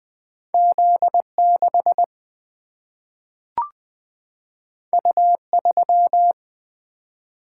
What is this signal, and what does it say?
Morse code, heard as a keyed tone, about 20 wpm, 700 Hz. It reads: Z 6 U 3
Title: Single-Letter Single-Number 20wpm